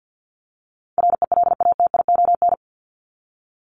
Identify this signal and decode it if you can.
Morse code, heard as a keyed tone, about 48 wpm, 700 Hz. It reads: RELATION
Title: ES:S Suffix - Top 100 Words with Base Words 50wpm